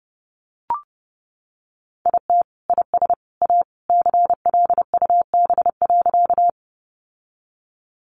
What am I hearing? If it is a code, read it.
Morse code, heard as a keyed tone, about 28 wpm, 700 Hz. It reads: IT IS A CLUB.
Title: Sentences from Top 1200 Words 30wpm